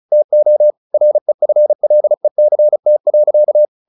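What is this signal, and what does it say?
Morse code, heard as a keyed tone, about 34 wpm, 600 Hz. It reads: TO REFLECT.
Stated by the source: Morse practice at 35 wpm from CW med Gnister, sending Ken Follett's novel Never in English.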